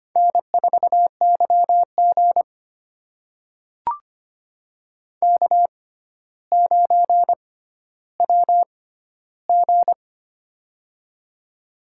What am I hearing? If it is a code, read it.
Morse code, heard as a keyed tone, about 25 wpm, spaced as for 13 wpm, 700 Hz. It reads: N4YG K9WG
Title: USA Call Signs - 4 Characters - 6x Character Spacing - 25wpm